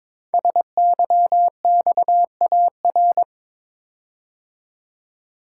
Morse code, heard as a keyed tone, about 22 wpm, 700 Hz. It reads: SYXAR
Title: Code Groups - 5 Letters - 22wpm